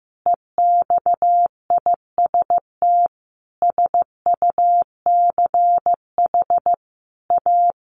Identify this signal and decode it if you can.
Morse code, heard as a keyed tone, about 15 wpm, 700 Hz. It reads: EXIST SUCH A
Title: Sentences from Top 1400 Words 15wpm